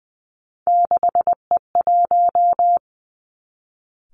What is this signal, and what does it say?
Morse code, heard as a keyed tone, about 20 wpm, 700 Hz. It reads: 6E1
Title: Three Challenging Letter-Number Combinations 20wpm